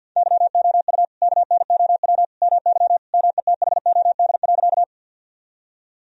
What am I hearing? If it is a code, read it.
Morse code, heard as a keyed tone, about 50 wpm, 700 Hz. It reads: YOU KNOW MY METHOD.
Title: Sentences from Top 1000 Words - Review Part 3 - 50wpm